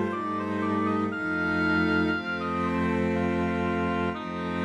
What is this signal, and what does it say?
There is no code.